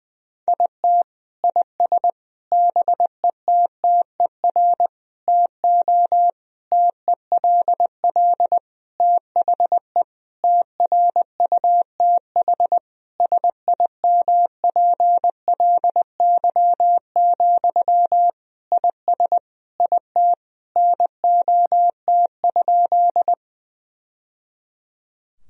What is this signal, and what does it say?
Morse code, heard as a keyed tone, about 20 wpm, 700 Hz. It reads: IT IS BETTER TO TELL THE TRUTH SIMPLY, IS IT NOT?
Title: Sentences from Top 1000 Words - Review Part 4 - 20wpm